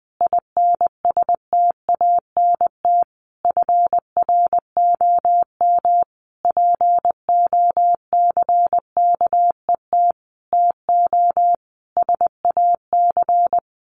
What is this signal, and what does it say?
Morse code, heard as a keyed tone, about 20 wpm, 700 Hz. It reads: INSTANT FROM POCKET TO SAC
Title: A Sound Of Thunder 20wpm